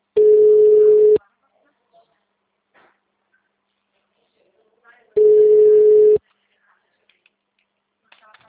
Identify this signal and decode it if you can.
Ringback tone (Europe, 425 Hz).